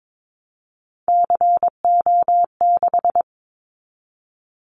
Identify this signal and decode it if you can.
Morse code, heard as a keyed tone, about 22 wpm, 700 Hz. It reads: CO6
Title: Callsign Prefixes for Top 100 DXCC - 22wpm